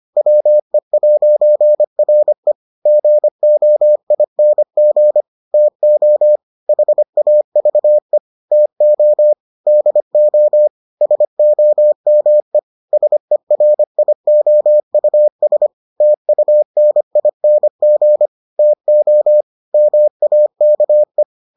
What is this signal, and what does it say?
Morse code, heard as a keyed tone, about 25 wpm, 600 Hz. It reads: WE'RE GOING TO HAVE TO DO SOME SERIOUS TUNING TO MAKE